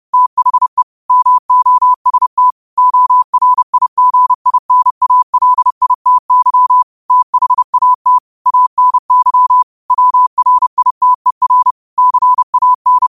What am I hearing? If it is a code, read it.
Morse code, heard as a keyed tone, about 30 wpm, 1000 Hz. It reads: THE MOST ORIGINALITY THAT ANY WRITER CAN